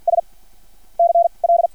Morse code, heard as a keyed tone, about 50 wpm, 680 Hz. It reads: S C L
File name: archived / CRSnet